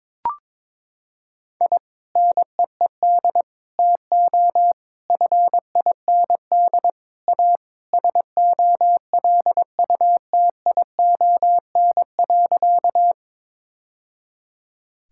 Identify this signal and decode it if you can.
Morse code, heard as a keyed tone, about 22 wpm, 700 Hz. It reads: I NEED TO FIND A SOLUTION.
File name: practice / 22wpm / Sentences with 3-Word Phrases - 22wpm